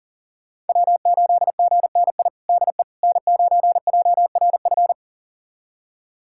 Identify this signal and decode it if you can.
Morse code, heard as a keyed tone, about 40 wpm, 700 Hz. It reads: W8GNI DE N9JRF
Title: USA Callsign DE Callsign - 5 Characters - 40wpm